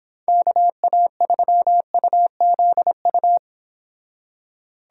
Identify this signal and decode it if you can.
Morse code, heard as a keyed tone, about 29 wpm, 700 Hz. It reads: KA3UZU